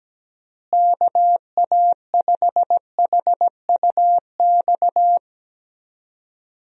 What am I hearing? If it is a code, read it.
Morse code, heard as a keyed tone, about 17 wpm, 700 Hz. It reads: KA5HUX
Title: USA Call Signs - 6 Characters - Repeated 2x - 17wpm